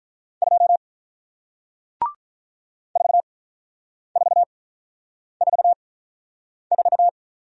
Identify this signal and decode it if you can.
Morse code, heard as a keyed tone, about 48 wpm, spaced as for 9 wpm, 700 Hz. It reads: 2 4444